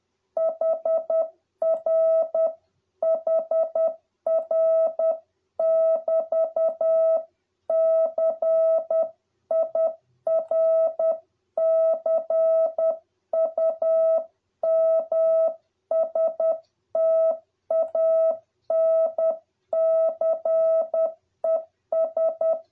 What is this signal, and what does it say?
Morse code, heard as a keyed tone, about 10 wpm, 650 Hz. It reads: HR HR <BT> CIRCUMSTANCES